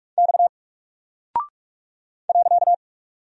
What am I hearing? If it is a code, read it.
Morse code, heard as a keyed tone, about 44 wpm, 700 Hz. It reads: X .
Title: Single Character (Letter:Number:Punctuation:Prosign) Alt 45wpm